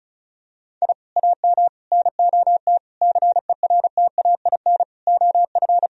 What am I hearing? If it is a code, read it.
Morse code, heard as a keyed tone, about 36 wpm, 700 Hz. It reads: I AM NOT CERTAIN OF
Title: Sentences from Top 1000 Words - Review Part 2 - 35wpm